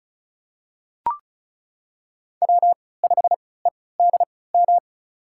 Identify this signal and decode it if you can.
Morse code, heard as a keyed tone, about 36 wpm, spaced as for 20 wpm, 700 Hz. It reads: W5EDM
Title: USA Call Signs - 5 Characters - 3x Character Spacing - 35wpm